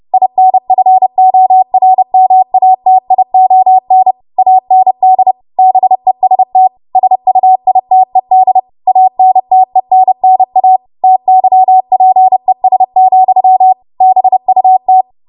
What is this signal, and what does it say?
Morse code, heard as a keyed tone, about 30 wpm, 750 Hz. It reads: INFORMATION AND BEST SUITED ANTENNA TYPES, BUT
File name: Morse / 30WPM